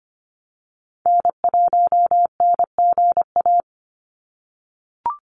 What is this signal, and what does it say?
Morse code, heard as a keyed tone, about 25 wpm, 700 Hz. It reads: N1NGA
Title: USA Call Signs 25wpm